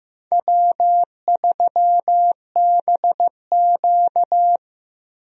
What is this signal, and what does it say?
Morse code, heard as a keyed tone, about 15 wpm, 700 Hz. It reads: W3BQ
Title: USA Call Signs - 4 Characters - Repeated 2x - 15wpm